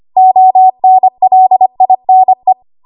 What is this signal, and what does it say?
Morse code, heard as a keyed tone, about 25 wpm, 750 Hz. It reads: ONLINE
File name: Archive / 25 WPM / 25WPM